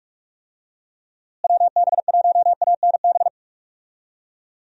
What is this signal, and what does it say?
Morse code, heard as a keyed tone, about 46 wpm, 700 Hz. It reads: WB1ANB
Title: USA Call Signs - 6 Characters - Repeated 3x - 45wpm